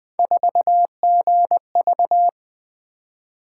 Morse code, heard as a keyed tone, about 24 wpm, 700 Hz. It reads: 4GV
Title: Speed-Racing - Three Letter-Number Combinations - 20wpm